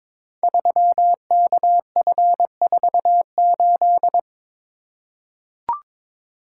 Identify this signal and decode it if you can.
Morse code, heard as a keyed tone, about 23 wpm, 700 Hz. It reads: 3KF48
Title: Code Groups - 5 Letter:Numbers - 22wpm